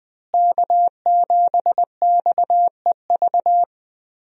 Morse code, heard as a keyed tone, about 20 wpm, 700 Hz. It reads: K7XEV
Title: USA Call Signs - 5 Characters - Repeated 2x - 20wpm